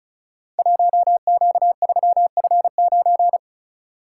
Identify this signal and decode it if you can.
Morse code, heard as a keyed tone, about 36 wpm, 700 Hz. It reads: 1Q3F9
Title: Code Groups - 5 Letter:Numbers - 35wpm